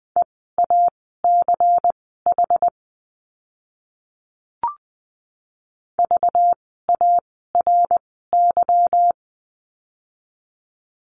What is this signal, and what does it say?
Morse code, heard as a keyed tone, about 20 wpm, spaced as for 14 wpm, 700 Hz. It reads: EACH VARY
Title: 192 Lesson 31f - Letter v - Words 20wpm